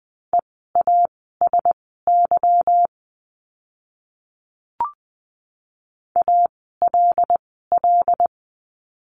Morse code, heard as a keyed tone, about 20 wpm, spaced as for 15 wpm, 700 Hz. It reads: EASY ALL
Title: 146 Lesson 24f - Letter y - Words 20wpm